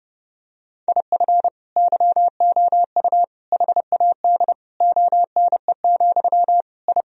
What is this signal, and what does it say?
Morse code, heard as a keyed tone, about 32 wpm, 700 Hz. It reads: IF YOU HAD ONE, I
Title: Sentences from Top 1000 Words - Review Part 3 - 30wpm